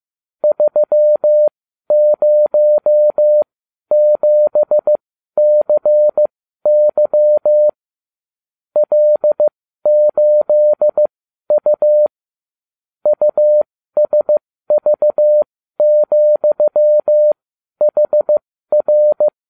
Morse code, heard as a keyed tone, about 15 wpm, spaced as for 12 wpm, 600 Hz. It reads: 307CY L8U USV,HR